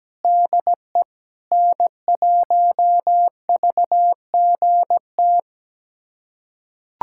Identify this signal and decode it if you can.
Morse code, heard as a keyed tone, about 17 wpm, 700 Hz. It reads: DE N1VGT
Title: USA Callsign DE Callsign - 5 Characters - 17wpm